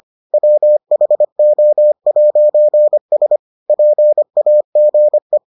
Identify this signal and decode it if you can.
Morse code, heard as a keyed tone, about 25 wpm, 600 Hz. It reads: WHO'S PAGE